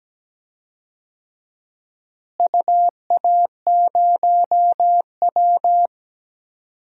Morse code, heard as a keyed tone, about 17 wpm, 700 Hz. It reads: UA0W
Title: DXCC-100 Call Signs - 4 Characters - Repeated 3x - 17wpm